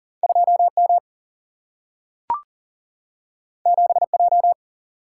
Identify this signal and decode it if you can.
Morse code, heard as a keyed tone, about 39 wpm, 700 Hz. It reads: 2 M 7 J
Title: Single-Letter Single-Number 40wpm